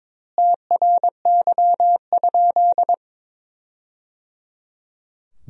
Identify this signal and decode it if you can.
Morse code, heard as a keyed tone, about 23 wpm, 700 Hz. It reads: TRY?
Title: 2x Word-Spacing - Sentences from Top 200 Words - 22wpm